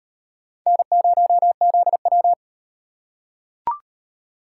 Morse code, heard as a keyed tone, about 37 wpm, 700 Hz. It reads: N0ZW